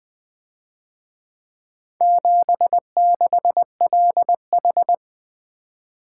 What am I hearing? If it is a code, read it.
Morse code, heard as a keyed tone, about 21 wpm, 700 Hz. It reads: 76LH
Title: Speed-Racing - Four Letter-Number Combinations - 20wpm